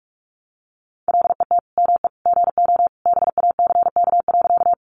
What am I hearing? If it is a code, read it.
Morse code, heard as a keyed tone, about 44 wpm, 700 Hz. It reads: LET ME GO BACK.
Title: Sentences from Top 300 Words 45wpm